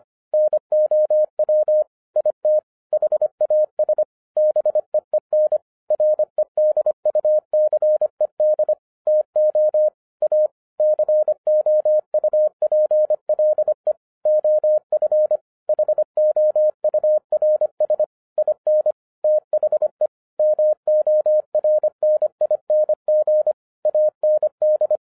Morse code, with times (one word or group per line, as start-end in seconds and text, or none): 0.3-1.8 NOW
2.2-2.6 IT
2.9-4.0 HAS
4.4-5.6 BEEN
5.9-8.7 REDUCED
9.1-9.9 TO
10.2-10.5 A
10.8-13.9 COUPLE
14.3-15.4 OF
15.7-18.0 HOURS
18.4-18.9 IN
19.2-20.1 THE
20.4-23.5 MORNING
23.9-25.0 AND